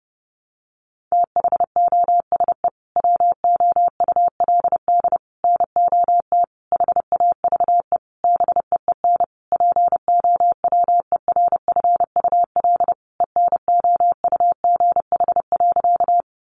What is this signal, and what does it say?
Morse code, heard as a keyed tone, about 30 wpm, 700 Hz. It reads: THOSE WOULD NOT HAVE BEEN POWERFUL ENOUGH.
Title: Sentences from Top 1300 Words 30wpm